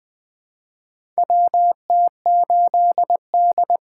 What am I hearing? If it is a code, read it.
Morse code, heard as a keyed tone, about 20 wpm, 700 Hz. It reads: WT8D
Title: USA Call Signs - 4 Characters - 8x Character Spacing - 20wpm